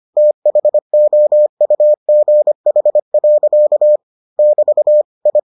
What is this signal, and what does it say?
Morse code, heard as a keyed tone, about 25 wpm, 600 Hz. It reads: THOUGH. <BT> I